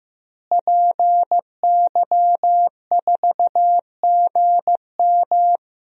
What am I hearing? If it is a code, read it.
Morse code, heard as a keyed tone, about 15 wpm, 700 Hz. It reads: PY4GM
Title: DXCC-100 Call Signs - 5 Characters - Repeated 3x - 15wpm